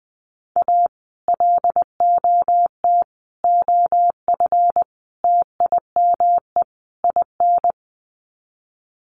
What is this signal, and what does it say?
Morse code, heard as a keyed tone, about 20 wpm, 700 Hz. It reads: A LOT OF TIME IN
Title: Sets of 5 Words Encore2 20wpm